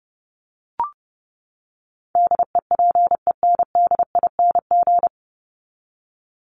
Morse code, heard as a keyed tone, about 30 wpm, 700 Hz. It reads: DEPENDING
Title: ING Suffix - Top 100 Words with Base Words 30wpm